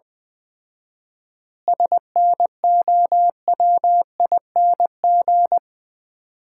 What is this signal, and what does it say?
Morse code, heard as a keyed tone, about 20 wpm, 700 Hz. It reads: SNOWING
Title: Common QSO Phrases - 4x Word-Spacing - 20wpm